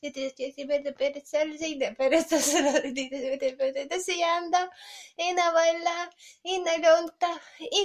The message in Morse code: none